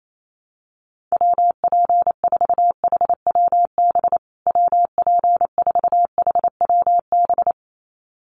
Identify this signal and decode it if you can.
Morse code, heard as a keyed tone, about 28 wpm, 700 Hz. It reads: WP4HWB WP4HWB
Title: USA Call Signs Repeated 2x 28wpm